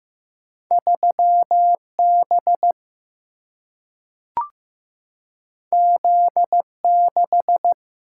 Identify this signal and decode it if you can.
Morse code, heard as a keyed tone, about 15 wpm, 700 Hz. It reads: 3B Z6
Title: Single-Letter Single-Number 15wpm